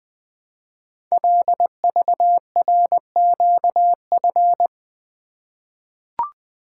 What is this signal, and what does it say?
Morse code, heard as a keyed tone, about 20 wpm, 700 Hz. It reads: LVRQF
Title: Code Groups - 5 Letters - 20wpm